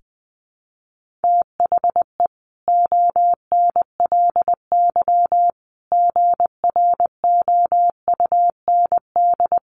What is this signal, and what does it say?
Morse code, heard as a keyed tone, about 19 wpm, 700 Hz. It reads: THE ONLY GROUND
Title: Sentences from Top 1400 Words 20wpm